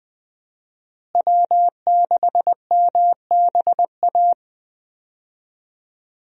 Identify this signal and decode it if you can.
Morse code, heard as a keyed tone, about 20 wpm, 700 Hz. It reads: W6MBA
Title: USA Call Signs - 5 Characters - Repeated 2x - 20wpm